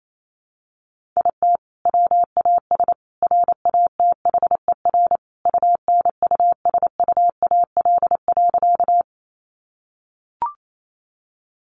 Morse code, heard as a keyed tone, about 28 wpm, 700 Hz. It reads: IT WAS RATHER UNUSUAL.
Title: Sentences from Top 1400 Words 28wpm